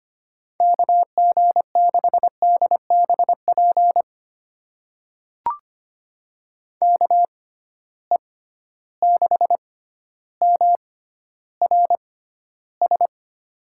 Morse code, heard as a keyed tone, about 25 wpm, spaced as for 13 wpm, 700 Hz. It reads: KG6DBP KE6MRS
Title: USA Call Signs - 6 Characters - 6x Character Spacing - 25wpm